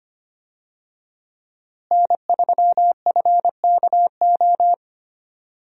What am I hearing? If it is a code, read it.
Morse code, heard as a keyed tone, about 25 wpm, 700 Hz. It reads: N3FKO
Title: USA Call Signs - 5 Characters - Repeated 2x - 25wpm